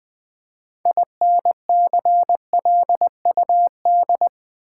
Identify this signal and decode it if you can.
Morse code, heard as a keyed tone, about 20 wpm, 700 Hz. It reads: INCLUD
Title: Build A Word - 2 Letters At A Time - 8-Letter Words - 20wpm